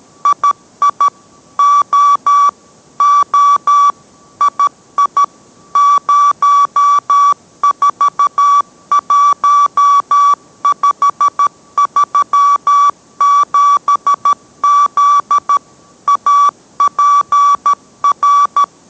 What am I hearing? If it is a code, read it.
Morse code, heard as a keyed tone, about 13 wpm, 1150 Hz. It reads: II O O II 041537Z APR